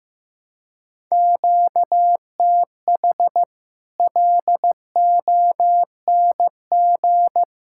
Morse code, heard as a keyed tone, about 15 wpm, 700 Hz. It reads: QTH LONG
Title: Practice QSO - Exchange 1 - Beginning - 15wpm